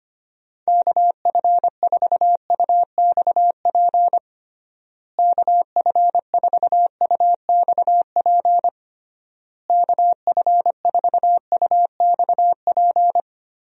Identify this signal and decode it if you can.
Morse code, heard as a keyed tone, about 25 wpm, 700 Hz. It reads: KF4UXP KF4UXP KF4UXP
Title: USA Call Signs - 6 Characters - Repeated 3x - 25wpm